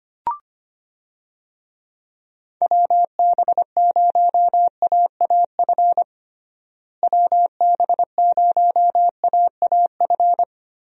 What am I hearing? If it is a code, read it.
Morse code, heard as a keyed tone, about 25 wpm, 700 Hz. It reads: WB0AAF WB0AAF
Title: USA Call Signs - 6 Characters - Repeated 3x - 25wpm